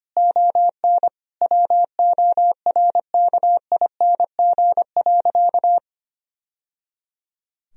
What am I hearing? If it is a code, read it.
Morse code, heard as a keyed tone, about 25 wpm, 700 Hz. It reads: ON WORKING.
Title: Sentences from Top 1000 Words - Review Part 4 - 25wpm